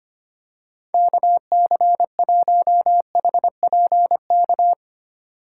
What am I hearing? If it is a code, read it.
Morse code, heard as a keyed tone, about 25 wpm, 700 Hz. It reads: KC1HPK
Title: USA Call Signs - 4x Character Spacing - 25wpm